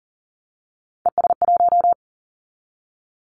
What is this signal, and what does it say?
Morse code, heard as a keyed tone, about 40 wpm, 700 Hz. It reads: ES1
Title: Callsign Prefixes for Top 50 DXCC - 40wpm